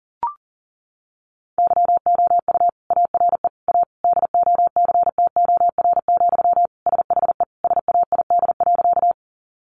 Morse code, heard as a keyed tone, about 40 wpm, 700 Hz. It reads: YOU ARE A DOCTOR, SHE SAID.
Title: Sentences from Top 1500 Words 40wpm